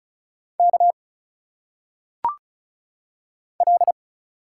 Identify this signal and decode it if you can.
Morse code, heard as a keyed tone, about 36 wpm, 700 Hz. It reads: K L
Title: Single Letters - Phonetic Pronunciation 35wpm